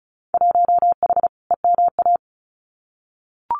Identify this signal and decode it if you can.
Morse code, heard as a keyed tone, about 34 wpm, 700 Hz. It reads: 1H EMA
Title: Winter Field Day Exchanges 35wpm